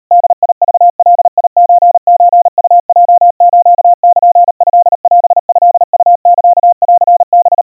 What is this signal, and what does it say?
Morse code (Japanese Wabun code), heard as a keyed tone, about 38 wpm, 700 Hz. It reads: ボクガソソウヲスルトオトウサンハ
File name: kawa / Morse sample / kamyu 140 cpm 700 Hz Crispy